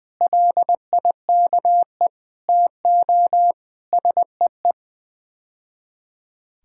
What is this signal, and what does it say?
Morse code, heard as a keyed tone, about 21 wpm, 700 Hz. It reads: LIKE TO SEE
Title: Build A Phrase - 5-Word Phrases - 20wpm